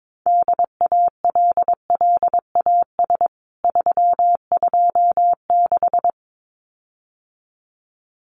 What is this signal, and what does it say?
Morse code, heard as a keyed tone, about 22 wpm, 700 Hz. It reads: DALLAS 326
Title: CWT Exchange 22wpm